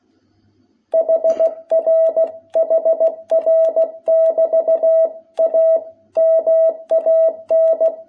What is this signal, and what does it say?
Morse code, heard as a keyed tone, about 16 wpm, 660 Hz. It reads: HR HR<BT>A MAN
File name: jh8cbh / 2021cw